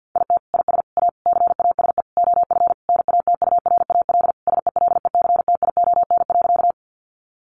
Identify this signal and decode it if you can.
Morse code, heard as a keyed tone, about 50 wpm, 700 Hz. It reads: IT IS A CASE OF NATURAL SELECTION.
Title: Sentences from Top 1400 Words 50wpm